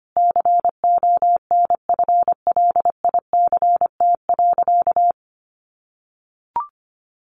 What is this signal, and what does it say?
Morse code, heard as a keyed tone, about 25 wpm, 700 Hz. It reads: CONFLICT.
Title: Sentences from Top 1400 Words 25wpm